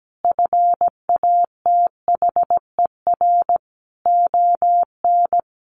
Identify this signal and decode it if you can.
Morse code, heard as a keyed tone, about 17 wpm, 700 Hz. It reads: FATHER ON
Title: Sentences from Top 500 Words 17wpm